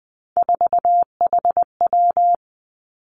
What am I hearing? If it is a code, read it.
Morse code, heard as a keyed tone, about 20 wpm, 700 Hz. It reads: 4HW
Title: Three Challenging Letter-Number Combinations 20wpm